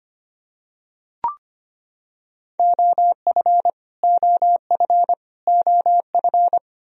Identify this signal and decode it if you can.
Morse code, heard as a keyed tone, about 25 wpm, 700 Hz. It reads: OF OF OF
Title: Top 100 Words Repeated 25wpm